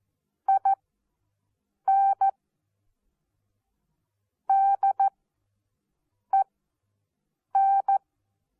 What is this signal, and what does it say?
Morse code, heard as a keyed tone, about 14 wpm, spaced as for 6 wpm, 780 Hz. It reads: IN DEN
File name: Onlinelehrgang / morsen / DJ4UF Morsen14